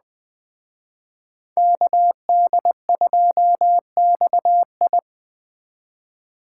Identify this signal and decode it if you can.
Morse code, heard as a keyed tone, about 20 wpm, 700 Hz. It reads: KD2XI
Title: Speed Racing - USA Call Signs 17wpm